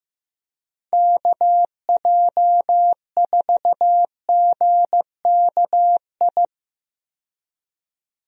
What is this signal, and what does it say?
Morse code, heard as a keyed tone, about 15 wpm, 700 Hz. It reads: KJ4GKI
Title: USA Call Signs - 6 Characters - 15wpm